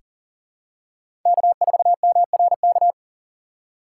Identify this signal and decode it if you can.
Morse code, heard as a keyed tone, about 40 wpm, 700 Hz. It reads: K4MRK
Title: USA Call Signs - 5 Characters - 4x Character Spacing - 40wpm